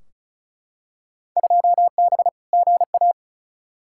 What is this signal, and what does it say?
Morse code, heard as a keyed tone, about 36 wpm, 700 Hz. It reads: 2B GA
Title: Field Day Exchanges Encore - 35wpm